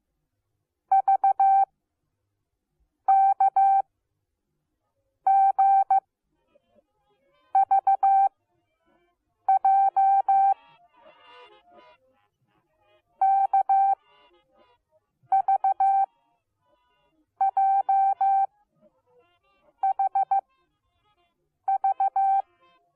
Morse code, heard as a keyed tone, about 15 wpm, spaced as for 5 wpm, 780 Hz. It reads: VKGVJ KVJHV